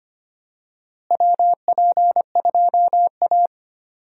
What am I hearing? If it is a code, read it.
Morse code, heard as a keyed tone, about 25 wpm, 700 Hz. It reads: WP2A
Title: USA Call Signs - 4 Characters - Repeated 3x - 25wpm